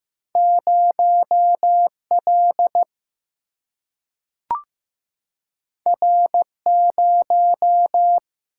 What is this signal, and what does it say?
Morse code, heard as a keyed tone, about 15 wpm, 700 Hz. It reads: 0 L R 0
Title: Single-Letter Single-Number 15wpm